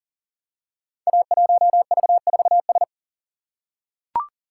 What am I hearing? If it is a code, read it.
Morse code, heard as a keyed tone, about 39 wpm, 700 Hz. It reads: A1V4S
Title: Code Groups - 5 Letter:Numbers - 40wpm